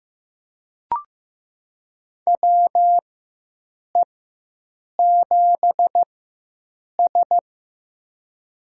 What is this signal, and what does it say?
Morse code, heard as a keyed tone, about 15 wpm, spaced as for 7 wpm, 700 Hz. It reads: WE7S
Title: USA Call Signs - 4x Character Spacing - 15wpm